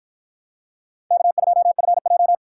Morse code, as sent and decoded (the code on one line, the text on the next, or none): -.- ..--- ..-. .---
K2FJ